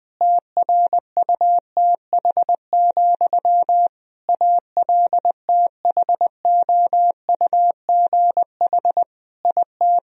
Morse code, heard as a keyed tone, about 20 wpm, 700 Hz. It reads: TRUTH, ALTHOUGH IT
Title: Sentences from Top 1000 Words - Review Part 3 - 20wpm